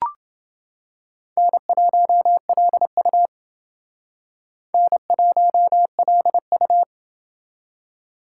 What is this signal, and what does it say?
Morse code, heard as a keyed tone, about 29 wpm, 700 Hz. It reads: N1LU N1LU